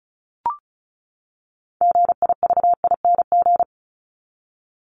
Morse code, heard as a keyed tone, about 34 wpm, 700 Hz. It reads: GIVING
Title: ING Suffix - Top 100 Words with Base Words 35wpm